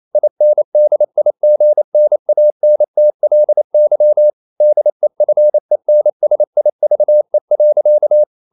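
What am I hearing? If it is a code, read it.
Morse code, heard as a keyed tone, about 28 wpm, 600 Hz. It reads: INDIGNANTLY DEFENSIVE.
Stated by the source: Morse code practice at 28 wpm, text of Ken Follett's novel Never in English, from CW med Gnister.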